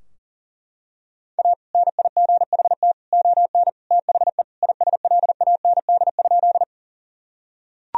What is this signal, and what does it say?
Morse code, heard as a keyed tone, about 37 wpm, 700 Hz. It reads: A NIGHT ON THE ISLAND?
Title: Sentences from Top 1000 Words - Review Part 3 - 40wpm